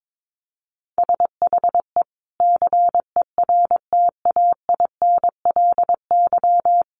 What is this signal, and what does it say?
Morse code, heard as a keyed tone, about 22 wpm, 700 Hz. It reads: SHE CERTAINLY
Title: Sentences from Top 1100 Words 22wpm